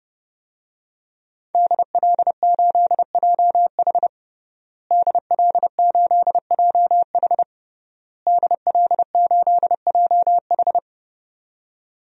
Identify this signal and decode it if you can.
Morse code, heard as a keyed tone, about 30 wpm, 700 Hz. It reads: DL8JH DL8JH DL8JH
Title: DXCC-100 Call Signs - 5 Characters - Repeated 3x - 30wpm